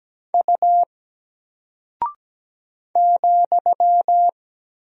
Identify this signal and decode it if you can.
Morse code, heard as a keyed tone, about 17 wpm, 700 Hz. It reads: U ,
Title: Single Character (Letter:Number:Punctuation:Prosign) 17wpm